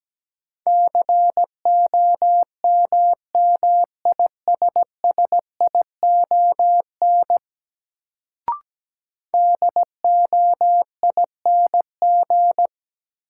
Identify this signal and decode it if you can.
Morse code, heard as a keyed tone, about 17 wpm, 700 Hz. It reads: COMMISSION DOING
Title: Sentences from Top 1400 Words - New Word Drill - 17wpm